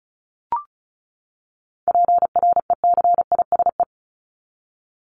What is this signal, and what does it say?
Morse code, heard as a keyed tone, about 34 wpm, 700 Hz. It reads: PRECISE
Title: LY Suffix - Top 100 Words with Base Words 35wpm